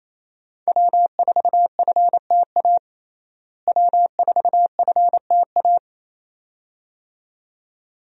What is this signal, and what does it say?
Morse code, heard as a keyed tone, about 28 wpm, 700 Hz. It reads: W4FTA W4FTA
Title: USA Call Signs - 5 Characters - Repeated 2x - 28wpm